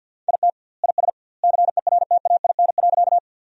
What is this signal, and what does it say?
Morse code, heard as a keyed tone, about 50 wpm, 700 Hz. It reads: IT IS CERTAIN.
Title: 2x Word-Spacing - Sentences from Top 500 Words - 50wpm